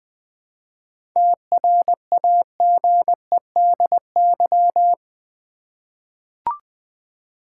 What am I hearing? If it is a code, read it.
Morse code, heard as a keyed tone, about 20 wpm, 700 Hz. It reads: TRAGEDY
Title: 7 Letter Words - First 3 Letters Given - 20wpm